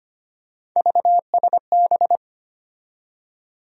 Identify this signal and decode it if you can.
Morse code, heard as a keyed tone, about 25 wpm, 700 Hz. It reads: VSB
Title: USA Call Sign Postfixes - 3 Letters 25wpm